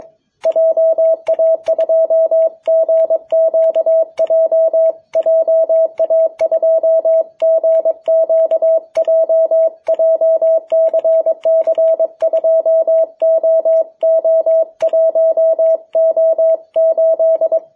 Morse code, heard as a keyed tone, about 22 wpm, 660 Hz. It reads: JA2GQJ JA2GQJ JCC2OO1O8
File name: jh8cbh / 2021cw